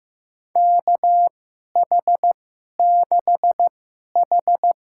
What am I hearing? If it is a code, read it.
Morse code, heard as a keyed tone, about 16 wpm, spaced as for 11 wpm, 700 Hz. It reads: KH6H
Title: USA Call Signs - 4 Characters - 2x Character Spacing - 15wpm